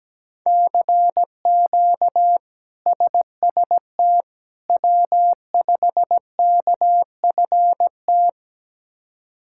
Morse code, heard as a keyed tone, about 17 wpm, 700 Hz. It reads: CQ SST W5KFT
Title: K1USN SST QSO Practice - 17wpm